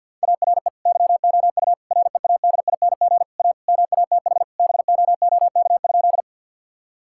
Morse code, heard as a keyed tone, about 50 wpm, 700 Hz. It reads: ARE YOU READING A MATH BOOK?
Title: Sentences from Top 1000 Words - Review Part 6 - 50wpm